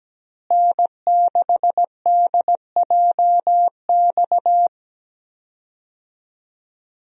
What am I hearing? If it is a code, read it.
Morse code, heard as a keyed tone, about 17 wpm, 700 Hz. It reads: N6DJX
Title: USA Call Signs - 5 Characters - Repeated 2x - 17wpm